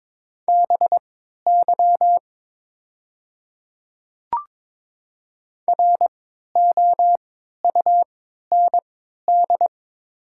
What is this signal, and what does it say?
Morse code, heard as a keyed tone, about 23 wpm, spaced as for 13 wpm, 700 Hz. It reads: BY ROUND
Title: 3x Character Spacing - Top 500 Words - 22wpm